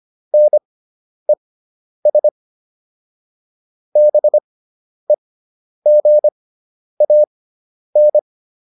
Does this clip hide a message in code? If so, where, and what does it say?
Yes: Morse code from 0.3 to 8.2 s, heard as a keyed tone, about 25 wpm, spaced as for 10 wpm, 600 Hz. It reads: NES BEGAN